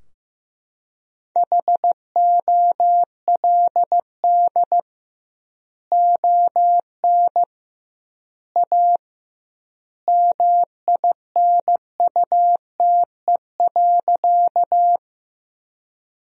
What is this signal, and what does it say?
Morse code, heard as a keyed tone, about 15 wpm, 700 Hz. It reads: HOLD ON A MINUTE.
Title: 2x Word-Spacing - Sentences from Top 500 Words - 15wpm